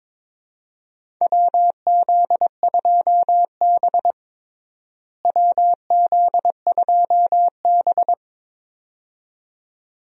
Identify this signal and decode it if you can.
Morse code, heard as a keyed tone, about 22 wpm, 700 Hz. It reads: WZ2B WZ2B
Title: USA Call Signs - 4 Characters - Repeated 2x - 22wpm